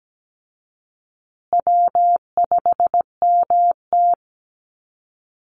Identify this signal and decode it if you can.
Morse code, heard as a keyed tone, about 17 wpm, 700 Hz. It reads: W5MT
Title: Easy Call Signs Repeated 3x - 17wpm